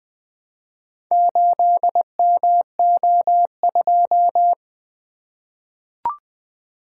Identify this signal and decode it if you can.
Morse code, heard as a keyed tone, about 20 wpm, 700 Hz. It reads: 8MO2
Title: Four Letter-Number Combinations - 20wpm